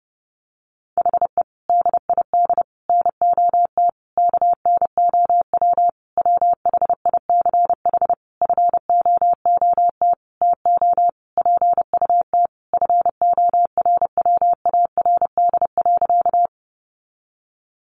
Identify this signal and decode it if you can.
Morse code, heard as a keyed tone, about 30 wpm, 700 Hz. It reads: HE DID NOT KNOW WHICH FOOT TO PUT FORWARD.
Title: Sentences from Top 1300 Words 30wpm